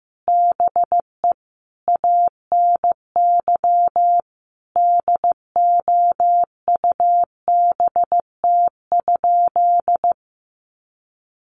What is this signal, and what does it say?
Morse code, heard as a keyed tone, about 15 wpm, 700 Hz. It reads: BE ANY DOUBT?
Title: Sentences from Top 1500 Words 15wpm